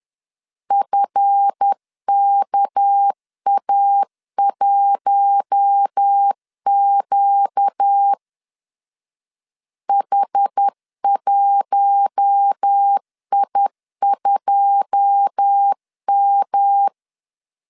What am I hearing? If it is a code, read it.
Morse code, heard as a keyed tone, about 10 wpm, 800 Hz. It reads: FKA1Q H1I2M
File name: CW / DET / 1 9 1500C S B 10wpm 4439